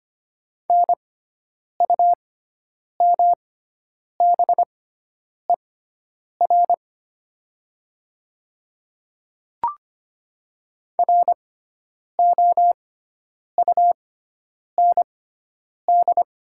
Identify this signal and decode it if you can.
Morse code, heard as a keyed tone, about 25 wpm, spaced as for 9 wpm, 700 Hz. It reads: NUMBER ROUND
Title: 6x Character Spacing - Top 200 Words - 25wpm